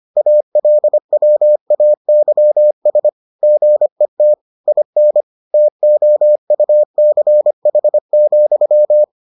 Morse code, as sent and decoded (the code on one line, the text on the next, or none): .- .-.. .-- .- -.-- ... / --. - / .. -. / - --- ..- -.-. .... --..--
ALWAYS GET IN TOUCH,